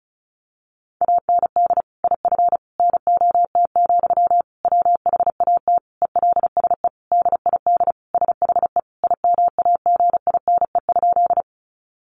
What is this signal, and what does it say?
Morse code, heard as a keyed tone, about 34 wpm, 700 Hz. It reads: AND IF NOT, WHAT ELSE DID SHE IMAGINE?
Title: Sentences from Top 1400 Words 35wpm